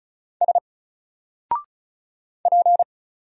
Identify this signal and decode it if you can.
Morse code, heard as a keyed tone, about 33 wpm, 700 Hz. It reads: S P
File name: Single-Letter-Number-35wpm